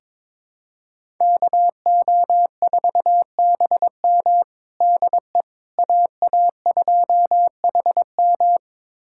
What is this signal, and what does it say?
Morse code, heard as a keyed tone, about 22 wpm, 700 Hz. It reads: KO4BM DE AA2HM
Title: USA Callsign DE Callsign - 5 Characters - 22wpm